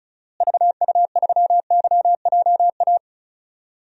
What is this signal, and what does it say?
Morse code, heard as a keyed tone, about 34 wpm, 700 Hz. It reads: VU3YJA
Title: DXCC-100 Call Signs - 6 Characters - Repeated 2x - 35wpm